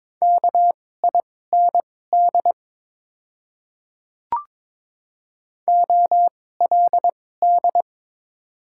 Morse code, heard as a keyed tone, about 22 wpm, spaced as for 16 wpm, 700 Hz. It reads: KIND OLD
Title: 2x Character Spacing - Top 200 Words - 22wpm